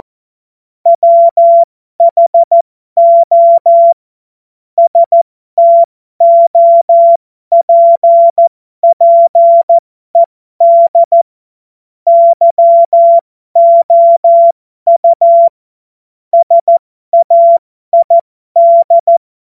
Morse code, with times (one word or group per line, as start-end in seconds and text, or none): 0.9-3.9 WHO
4.8-11.2 STOPPED
12.1-15.5 YOU
16.3-19.2 SAID